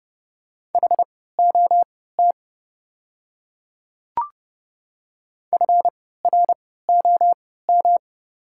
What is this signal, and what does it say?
Morse code, heard as a keyed tone, about 30 wpm, spaced as for 17 wpm, 700 Hz. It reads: HOT FROM
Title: 3x Character Spacing - Top 100 Words - 30wpm